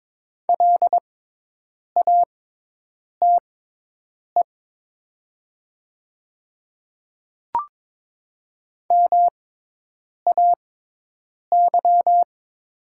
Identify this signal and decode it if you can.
Morse code, heard as a keyed tone, about 22 wpm, spaced as for 8 wpm, 690 Hz. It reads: LATE MAY
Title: 6x Character Spacing - Top 300 Words - 22wpm